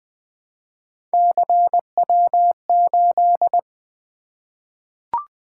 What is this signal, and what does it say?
Morse code, heard as a keyed tone, about 20 wpm, 700 Hz. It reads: CW8
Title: Three Letter-Number Combinations - 20wpm